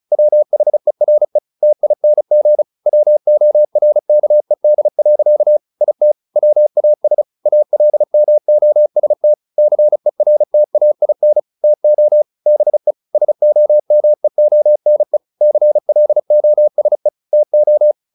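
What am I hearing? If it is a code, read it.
Morse code, heard as a keyed tone, about 34 wpm, 600 Hz. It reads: WHERE TING WORKED. IT WAS ALMOST CERTAIN TO BE SOMEONE CLOSE TO
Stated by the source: Morse practice at 35 wpm from CW med Gnister, sending Ken Follett's novel Never in English.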